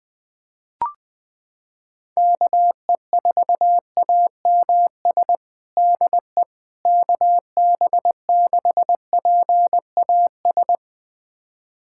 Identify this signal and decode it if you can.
Morse code, heard as a keyed tone, about 20 wpm, 700 Hz. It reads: KE4AMS DE KB6PAS
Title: USA Callsign DE Callsign 20wpm